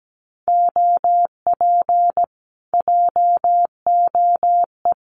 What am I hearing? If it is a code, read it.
Morse code, heard as a keyed tone, about 17 wpm, 700 Hz. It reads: OP JOE
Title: Common QSO Phrases Encore 17wpm